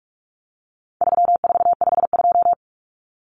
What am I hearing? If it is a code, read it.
Morse code, heard as a keyed tone, about 44 wpm, 700 Hz. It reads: 3452
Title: Numbers - 4-Digit 45wpm